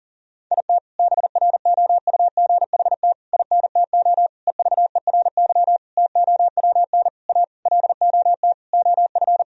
Morse code, heard as a keyed tone, about 40 wpm, 700 Hz. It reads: IT BROUGHT INTO EVERY TOWN A LOT OF
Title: Sentences from Top 1000 Words - Review Part 3 - 40wpm